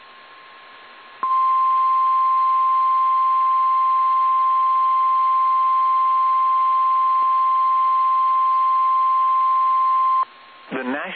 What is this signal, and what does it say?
Emergency Alert System (SAME) tones (the 1050 Hz attention tone), with speech. It is heard from 1.2 to 10.3 s.